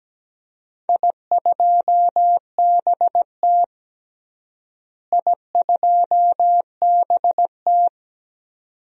Morse code, heard as a keyed tone, about 17 wpm, 700 Hz. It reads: I2BT I2BT
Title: DXCC-100 Call Signs - 4 Characters - Repeated 2x - 17wpm